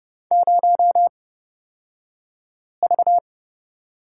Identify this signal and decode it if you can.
Morse code, heard as a keyed tone, about 30 wpm, spaced as for 5 wpm, 700 Hz. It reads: 0V
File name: Lesson 37d - Letter x - Two Characters 30wpm